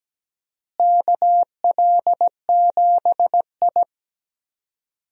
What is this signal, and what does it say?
Morse code, heard as a keyed tone, about 17 wpm, 700 Hz. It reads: KL7I
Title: DXCC-100 Call Signs - 4 Characters - Repeated 3x - 17wpm